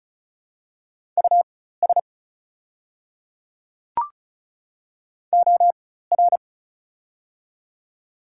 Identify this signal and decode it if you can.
Morse code, heard as a keyed tone, about 36 wpm, spaced as for 16 wpm, 700 Hz. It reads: US OR
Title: 4x Character Spacing - Top 200 Words - 35wpm